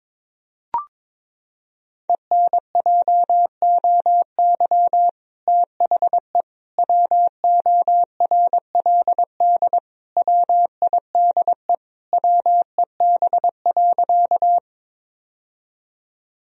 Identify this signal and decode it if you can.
Morse code, heard as a keyed tone, about 22 wpm, 700 Hz. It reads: ENJOY THE WORLD WIDE WEB.
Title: Sentences from Top 1000 Words - Review Part 4 - 22wpm